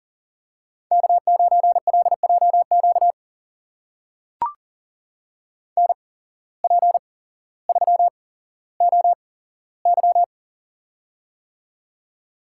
Morse code, heard as a keyed tone, about 39 wpm, spaced as for 17 wpm, 700 Hz. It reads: K9LJQ NP3OY
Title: USA Call Signs - 5 Characters - 8x Character Spacing - 40wpm